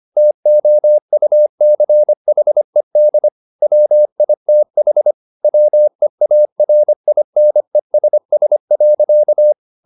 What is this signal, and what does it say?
Morse code, heard as a keyed tone, about 25 wpm, 600 Hz. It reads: TOUCHED WITH WEARINESS.